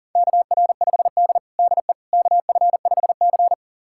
Morse code, heard as a keyed tone, about 40 wpm, 700 Hz. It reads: KR5D DE KF5C
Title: USA Callsign DE Callsign - 4 Characters - 40wpm